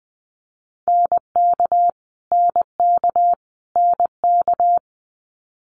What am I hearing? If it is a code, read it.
Morse code, heard as a keyed tone, about 20 wpm, 700 Hz. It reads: NK NK NK
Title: Multi-letter Phonemes Repeated 3x 20wpm